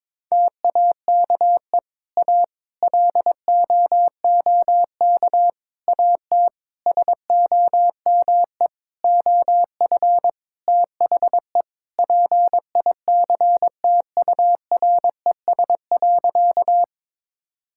Morse code, heard as a keyed tone, about 22 wpm, 700 Hz. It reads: TAKE A LOOK AT SOME OF THE PICTURES.
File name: Sentences with 7-Word Phrases - 22wpm